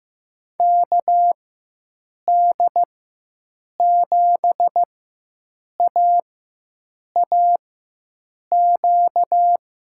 Morse code, heard as a keyed tone, about 15 wpm, spaced as for 7 wpm, 700 Hz. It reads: KD7AAQ
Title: USA Call Signs - 6 Characters - 4x Character Spacing - 15wpm